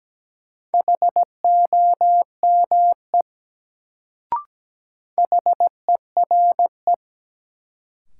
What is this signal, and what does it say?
Morse code, heard as a keyed tone, about 17 wpm, 700 Hz. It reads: HOME HERE
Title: Sentences from Top 200 Words - New Word Drill - 17wpm